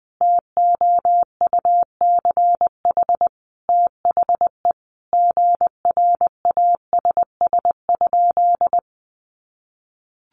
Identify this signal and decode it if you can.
Morse code, heard as a keyed tone, about 20 wpm, 700 Hz. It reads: TOUCH THE GRASS?
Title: A Sound Of Thunder 20wpm